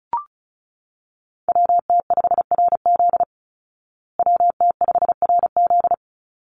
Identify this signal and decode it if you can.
Morse code, heard as a keyed tone, about 34 wpm, 700 Hz. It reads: WT5RZ WT5RZ
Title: Call Signs Repeated 3x - 35wpm